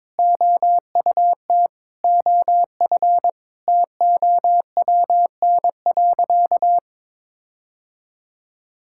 Morse code, heard as a keyed tone, about 22 wpm, 700 Hz. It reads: OUT OF TOWN.
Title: Sentences from Top 1600 Words 22wpm